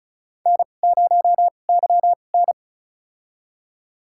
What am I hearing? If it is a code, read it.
Morse code, heard as a keyed tone, about 36 wpm, spaced as for 25 wpm, 700 Hz. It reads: N0YN
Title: USA Call Signs - 4 Characters - 2x Character Spacing - 35wpm